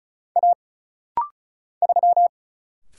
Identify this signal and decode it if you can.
Morse code, heard as a keyed tone, about 34 wpm, 700 Hz. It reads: A 3
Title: Single Letter-Number Alt - Rapid-Fire 35wpm